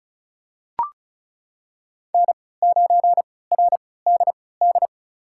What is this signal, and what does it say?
Morse code, heard as a keyed tone, about 34 wpm, spaced as for 20 wpm, 700 Hz. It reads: N9RDD
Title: USA Call Signs - 5 Characters - 3x Character Spacing - 35wpm